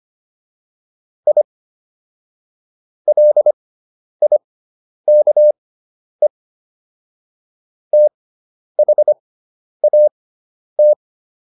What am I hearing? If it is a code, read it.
Morse code, heard as a keyed tone, about 25 wpm, spaced as for 10 wpm, 600 Hz. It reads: I LIKE THAT